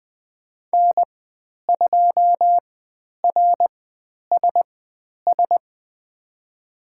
Morse code, heard as a keyed tone, about 20 wpm, spaced as for 10 wpm, 700 Hz. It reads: N2RSS
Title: 095 Lesson 16g - Letter c - Call Signs 20wpm